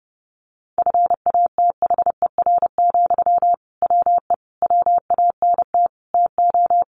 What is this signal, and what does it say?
Morse code, heard as a keyed tone, about 30 wpm, 700 Hz. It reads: FATHER, WE WANT TO
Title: Sentences from Top 700 Words 30wpm